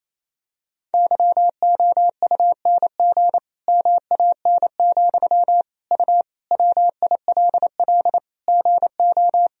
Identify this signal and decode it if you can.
Morse code, heard as a keyed tone, about 28 wpm, 700 Hz. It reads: YOUNG MAN, U WILL GO
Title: Sentences from Top 400 Words and Abbreviations 28wpm